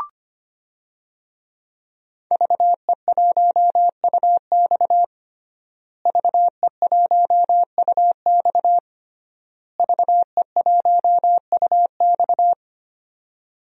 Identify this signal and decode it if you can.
Morse code, heard as a keyed tone, about 25 wpm, 700 Hz. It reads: VE1UX VE1UX VE1UX
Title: DXCC-100 Call Signs - 5 Characters - Repeated 3x - 25wpm